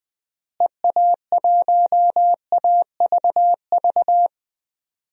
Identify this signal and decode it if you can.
Morse code, heard as a keyed tone, about 20 wpm, 700 Hz. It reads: EA1AVV
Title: DXCC-100 Call Signs - 6 Characters - Repeated 3x - 20wpm